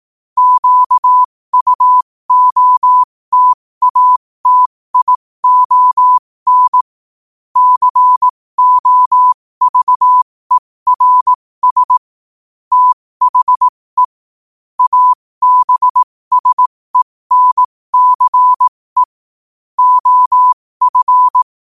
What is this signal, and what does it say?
Morse code, heard as a keyed tone, about 18 wpm, spaced as for 15 wpm, 1000 Hz. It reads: QUOTATION COVERS THE ABSENCE OF
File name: 15wpm